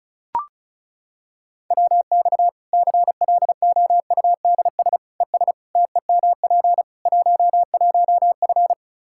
Morse code, heard as a keyed tone, about 36 wpm, 700 Hz. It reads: WX CLOUDS ES TEMP 11F
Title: Practice QSO - Exchange 2 - Info - 35wpm